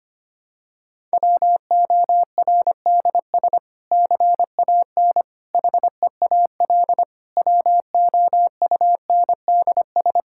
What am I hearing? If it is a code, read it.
Morse code, heard as a keyed tone, about 25 wpm, 700 Hz. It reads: WORDS CAN HEAL WOUNDS
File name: practice / 25wpm / Difficult Sentences with Hints - 25wpm